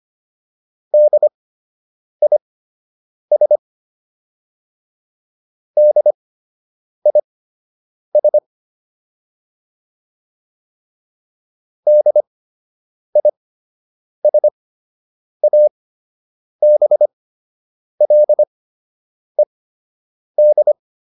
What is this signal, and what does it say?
Morse code, heard as a keyed tone, about 25 wpm, spaced as for 8 wpm, 600 Hz. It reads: DIS DIS DISABLED